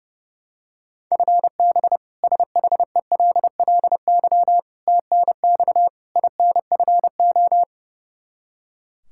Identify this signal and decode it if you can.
Morse code, heard as a keyed tone, about 30 wpm, 700 Hz. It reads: FB SHELLY TNX INFO
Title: K1USN SST QSO Practice 30wpm